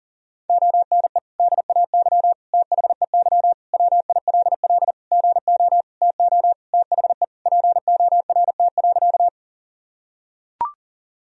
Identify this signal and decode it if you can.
Morse code, heard as a keyed tone, about 40 wpm, 700 Hz. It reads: ONE DAY THEY WILL GO TO THE PORT.
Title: Sentences from Top 1000 Words - Review Part 1 - 40wpm